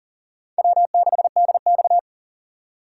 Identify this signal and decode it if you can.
Morse code, heard as a keyed tone, about 40 wpm, 700 Hz. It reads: W6DX
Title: USA Call Signs - 4x Character Spacing - 40wpm